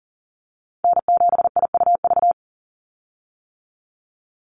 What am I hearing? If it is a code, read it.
Morse code, heard as a keyed tone, about 40 wpm, 690 Hz. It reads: N7IUV
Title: USA Call Signs 40wpm